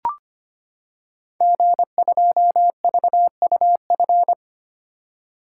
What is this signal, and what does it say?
Morse code, heard as a keyed tone, about 25 wpm, 700 Hz. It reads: G2VUF
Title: Code Groups - 5 Letter:Numbers - 25wpm